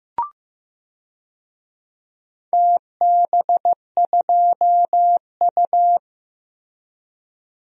Morse code, heard as a keyed tone, about 15 wpm, 700 Hz. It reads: TB2U
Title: DXCC-100 Call Signs - 4 Characters - Repeated 2x - 15wpm